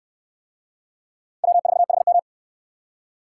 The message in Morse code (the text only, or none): W4HM